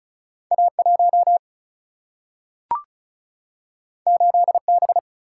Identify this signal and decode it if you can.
Morse code, heard as a keyed tone, about 36 wpm, 700 Hz. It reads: A1 8B
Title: Single-Letter Single-Number 35wpm